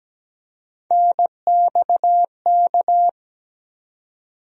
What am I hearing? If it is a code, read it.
Morse code, heard as a keyed tone, about 17 wpm, 700 Hz. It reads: NXK
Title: USA Call Sign Postfixes - 3 Letters 17wpm